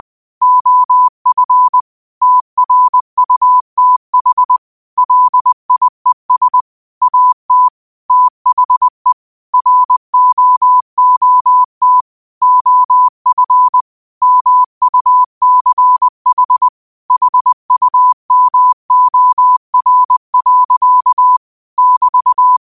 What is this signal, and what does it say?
Morse code, heard as a keyed tone, about 20 wpm, 1000 Hz. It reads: OF TRUTH LIES AT THE ROOT OF MUCH HUMOR. <BT>